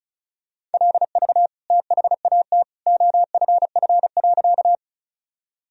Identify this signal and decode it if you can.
Morse code, heard as a keyed tone, about 34 wpm, 700 Hz. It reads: LV THAT OFF.
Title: Sentences from Top 400 Words and Abbreviations 35wpm